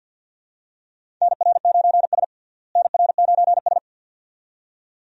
Morse code, heard as a keyed tone, about 49 wpm, 700 Hz. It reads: NR9S NR9S
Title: USA Call Signs - 4 Characters - Repeated 2x - 50wpm